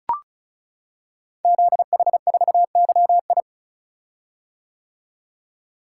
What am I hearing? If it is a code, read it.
Morse code, heard as a keyed tone, about 36 wpm, 700 Hz. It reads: ZH4YI
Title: Code Groups - 5 Letter:Numbers - 35wpm